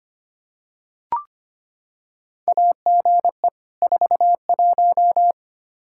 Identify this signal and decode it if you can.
Morse code, heard as a keyed tone, about 25 wpm, 700 Hz. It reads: AGE 41
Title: K1USN SST QSO Practice 25wpm